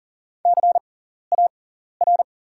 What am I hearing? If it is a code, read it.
Morse code, heard as a keyed tone, about 40 wpm, spaced as for 14 wpm, 700 Hz. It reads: CAR
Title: 6x Character Spacing - Top 500 Words - 40wpm